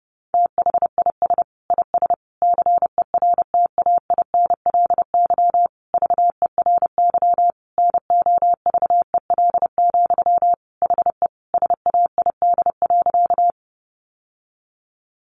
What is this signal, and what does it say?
Morse code, heard as a keyed tone, about 30 wpm, 700 Hz. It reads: THIS IS CERTAINLY VERY NOVEL, HE SAID.
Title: Sentences from Top 1300 Words 30wpm